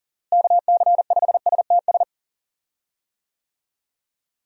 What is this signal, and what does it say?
Morse code, heard as a keyed tone, about 40 wpm, 700 Hz. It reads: KC5STS
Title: USA Call Signs - 6 Characters - Repeated 3x - 40wpm